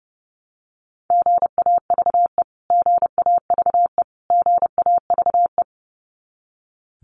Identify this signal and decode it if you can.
Morse code, heard as a keyed tone, about 30 wpm, 700 Hz. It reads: GAVE GAVE GAVE
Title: Top 500 Words Repeated 30wpm